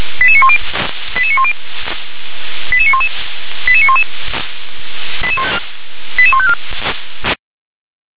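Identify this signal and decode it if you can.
Five-tone selective call (ZVEI), heard as quick runs of five tones, tones of about 70 ms. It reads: 88910 88910 88911 88911 88154
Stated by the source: ZVEI five-tone selective call received on SDR.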